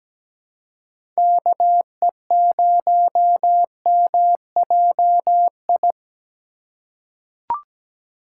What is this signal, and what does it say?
Morse code, heard as a keyed tone, about 17 wpm, 700 Hz. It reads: KE0MJI
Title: USA Call Signs - 6 Characters - 2x Character Spacing - 17wpm